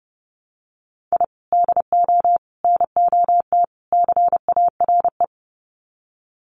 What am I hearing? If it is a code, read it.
Morse code, heard as a keyed tone, about 29 wpm, 700 Hz. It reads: I DO NOT CARE
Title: Sets of 4 Words 30wpm